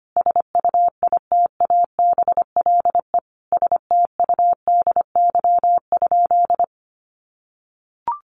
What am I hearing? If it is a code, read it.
Morse code, heard as a keyed tone, about 25 wpm, 700 Hz. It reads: SUITABLE STUDY?
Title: Sentences from Top 1400 Words 25wpm